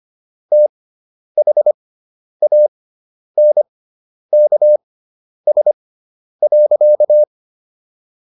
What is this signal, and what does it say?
Morse code, heard as a keyed tone, about 25 wpm, spaced as for 10 wpm, 600 Hz. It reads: THANKS.